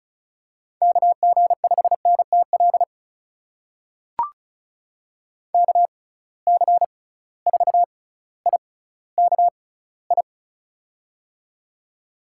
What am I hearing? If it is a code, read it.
Morse code, heard as a keyed tone, about 36 wpm, spaced as for 18 wpm, 700 Hz. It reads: KG5NTL KC4IKI
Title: USA Call Signs - 6 Characters - 6x Character Spacing - 35wpm